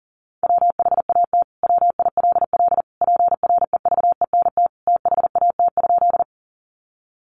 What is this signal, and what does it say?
Morse code, heard as a keyed tone, about 40 wpm, 700 Hz. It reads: WHAT WILL PREVENT THAT?
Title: Sentences from Top 1100 Words 40wpm